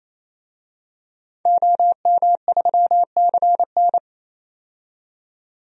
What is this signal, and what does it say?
Morse code, heard as a keyed tone, about 28 wpm, 700 Hz. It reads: OM3CN
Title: DXCC-100 Call Signs - 5 Characters - 28wpm